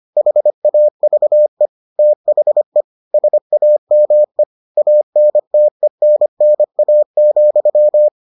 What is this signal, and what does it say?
Morse code, heard as a keyed tone, about 25 wpm, 600 Hz. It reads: HAVE THE SAME ANTENNA,